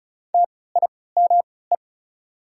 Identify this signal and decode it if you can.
Morse code, heard as a keyed tone, about 36 wpm, spaced as for 20 wpm, 690 Hz. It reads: TIME
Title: 3x Character Spacing - Top 500 Words - 35wpm